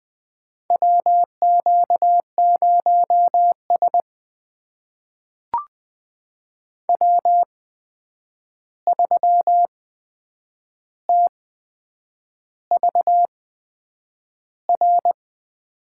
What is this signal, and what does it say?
Morse code, heard as a keyed tone, about 20 wpm, spaced as for 8 wpm, 700 Hz. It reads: WQ0S W3TVR
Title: USA Call Signs - 8x Character Spacing - 20wpm